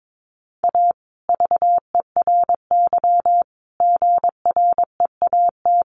Morse code, heard as a keyed tone, about 22 wpm, 700 Hz. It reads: A VERY GREAT
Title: Sentences from Top 1400 Words 22wpm